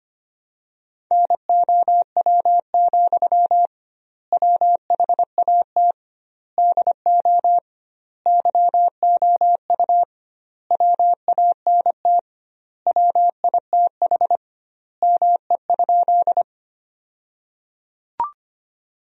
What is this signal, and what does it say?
Morse code, heard as a keyed tone, about 25 wpm, 700 Hz. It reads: NOW, WHAT DO YOU WANT WITH ME?
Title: 2x Word-Spacing - Sentences from Top 200 Words - 25wpm